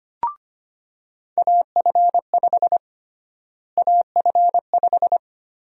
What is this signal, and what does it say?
Morse code, heard as a keyed tone, about 25 wpm, 700 Hz. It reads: AF5 AF5
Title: USA Call Sign Prefixes - Repeated 2x - 25wpm